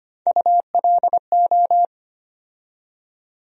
Morse code, heard as a keyed tone, about 25 wpm, 700 Hz. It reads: ULO
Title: USA Call Sign Postfixes - 3 Letters 25wpm